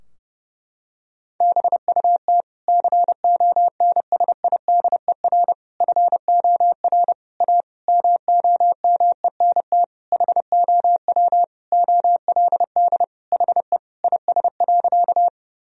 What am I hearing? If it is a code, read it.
Morse code, heard as a keyed tone, about 32 wpm, 700 Hz. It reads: BUT CONSIDER FOR A MOMENT HOW OLD HE IS.
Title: Sentences from Top 1000 Words - Review Part 4 - 30wpm